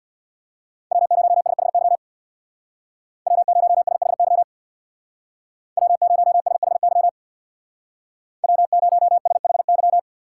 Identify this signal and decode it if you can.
Morse code, heard as a keyed tone, about 64 wpm, 700 Hz. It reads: W0SHY W0SHY W0SHY W0SHY